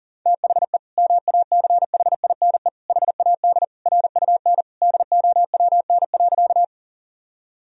Morse code, heard as a keyed tone, about 40 wpm, 700 Hz. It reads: THE MACHINE HAD RUN DOWN.
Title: Sentences from Top 1000 Words - Review Part 2 - 40wpm